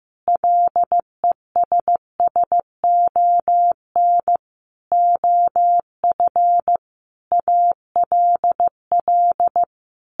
Morse code, heard as a keyed tone, about 15 wpm, 700 Hz. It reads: LESSON OF ALL
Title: Sentences from Top 1200 Words 15wpm